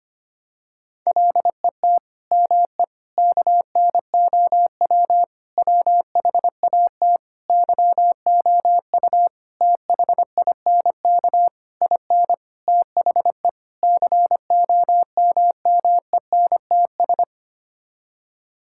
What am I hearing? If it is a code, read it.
Morse code, heard as a keyed tone, about 25 wpm, 700 Hz. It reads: LET ME KNOW WHAT YOU THINK IN THE COMMENTS
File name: Sets of 9 Words - 25wpm